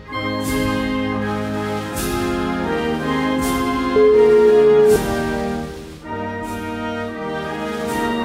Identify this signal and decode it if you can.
Ringback tone (Europe, 425 Hz), with noise.